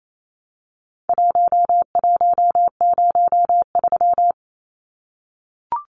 Morse code, heard as a keyed tone, about 28 wpm, 700 Hz. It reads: 1103
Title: Numbers - Contest Style - 4-Digit 28wpm